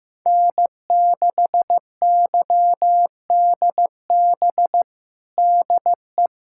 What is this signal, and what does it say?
Morse code, heard as a keyed tone, about 15 wpm, 700 Hz. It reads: N6YDB DE
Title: USA Callsign DE Callsign - 5 Characters - 15wpm